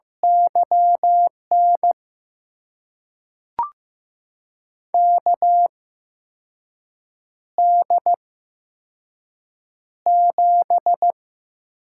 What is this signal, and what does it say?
Morse code, heard as a keyed tone, about 15 wpm, spaced as for 5 wpm, 700 Hz. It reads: YN KD7
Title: USA Call Signs - 6 Characters - 8x Character Spacing - 15wpm